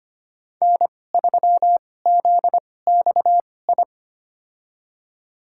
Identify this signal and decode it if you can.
Morse code, heard as a keyed tone, about 25 wpm, spaced as for 18 wpm, 700 Hz. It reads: N3ZXI
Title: USA Call Signs - 5 Characters - 2x Character Spacing - 25wpm